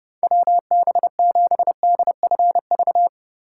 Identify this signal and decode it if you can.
Morse code, heard as a keyed tone, about 29 wpm, 700 Hz. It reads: WB7DFV
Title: Speed Racing - USA Call Signs 30wpm